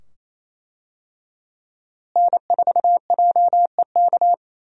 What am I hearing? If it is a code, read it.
Morse code, heard as a keyed tone, about 28 wpm, 700 Hz. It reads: N4JEK
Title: USA Call Signs - 5 Characters - Repeated 2x - 28wpm